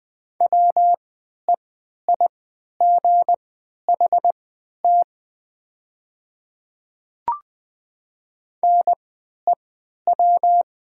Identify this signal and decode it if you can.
Morse code, heard as a keyed tone, about 20 wpm, spaced as for 11 wpm, 700 Hz. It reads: WEIGHT NEW
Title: 3x Character Spacing - Top 500 Words - 20wpm